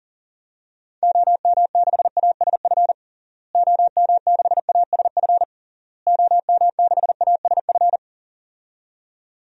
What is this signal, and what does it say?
Morse code, heard as a keyed tone, about 37 wpm, 700 Hz. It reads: OM6ASF OM6ASF OM6ASF
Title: DXCC-100 Call Signs - 6 Characters - Repeated 3x - 40wpm